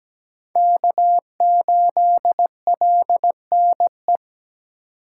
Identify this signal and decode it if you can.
Morse code, heard as a keyed tone, about 17 wpm, 700 Hz. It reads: K8LNE
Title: USA Call Signs - 5 Characters - 3x Character Spacing - 17wpm